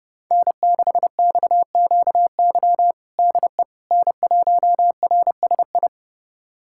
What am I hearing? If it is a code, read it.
Morse code, heard as a keyed tone, about 30 wpm, 700 Hz. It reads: N6XQY DE N1RSI
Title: USA Callsign DE Callsign - 5 Characters - 30wpm